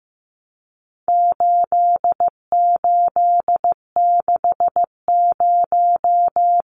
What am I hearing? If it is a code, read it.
Morse code, heard as a keyed tone, about 15 wpm, 700 Hz. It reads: 8860
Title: Numbers - 4-Digit 15wpm